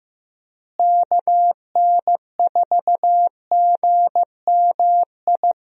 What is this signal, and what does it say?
Morse code, heard as a keyed tone, about 15 wpm, 700 Hz. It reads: KN4GMI
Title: USA Call Signs - 6 Characters - Repeated 3x - 15wpm